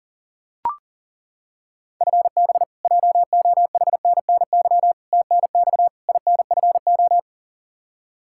Morse code, heard as a keyed tone, about 40 wpm, 700 Hz. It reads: FB JOHNNY TNX INFO
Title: Practice QSO - Exchange 3 - More Info - 40wpm